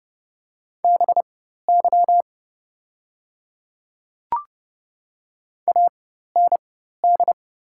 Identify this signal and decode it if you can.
Morse code, heard as a keyed tone, about 30 wpm, spaced as for 14 wpm, 690 Hz. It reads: BY AND
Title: 4x Character Spacing - Top 100 Words - 30wpm